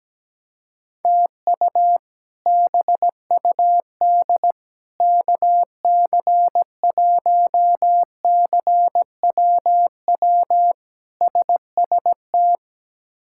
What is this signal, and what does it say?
Morse code, heard as a keyed tone, about 17 wpm, 700 Hz. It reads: TU BUD KC1CWW SST
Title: K1USN SST QSO Practice - 17wpm